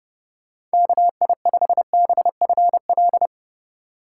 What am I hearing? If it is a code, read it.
Morse code, heard as a keyed tone, about 31 wpm, 700 Hz. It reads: KI5BFL